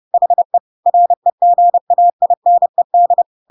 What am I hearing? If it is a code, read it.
Morse code, heard as a keyed tone, about 30 wpm, 700 Hz. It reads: HE REGAINED
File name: Pinocchio / Pinocchio ch1 30 30 ik1hge